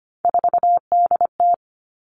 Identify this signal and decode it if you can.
Morse code, heard as a keyed tone, about 25 wpm, 700 Hz. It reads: 4DT
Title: All Cut Numbers - 3-Digit 25wpm